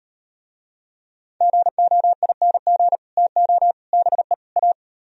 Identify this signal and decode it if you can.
Morse code, heard as a keyed tone, about 37 wpm, 700 Hz. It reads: GOING TO BE A